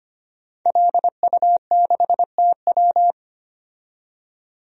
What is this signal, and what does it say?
Morse code, heard as a keyed tone, about 25 wpm, 700 Hz. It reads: LU6TW
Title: DXCC-100 Call Signs - 5 Characters - 25wpm